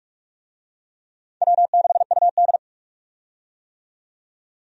Morse code, heard as a keyed tone, about 42 wpm, 700 Hz. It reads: W6UD
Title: USA Call Signs - 4 Characters - 45wpm